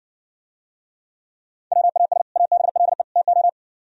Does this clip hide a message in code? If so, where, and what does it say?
Yes: Morse code from 1.7 to 3.5 s, heard as a keyed tone, about 60 wpm, 700 Hz. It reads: WAS ABLE TO